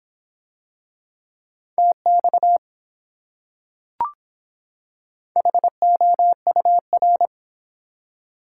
Morse code, heard as a keyed tone, about 24 wpm, 700 Hz. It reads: TX HOUR